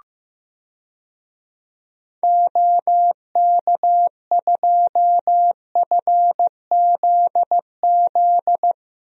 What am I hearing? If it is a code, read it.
Morse code, heard as a keyed tone, about 15 wpm, 700 Hz. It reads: OK2FZZ
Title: DXCC-100 Call Signs - 6 Characters - Repeated 3x - 15wpm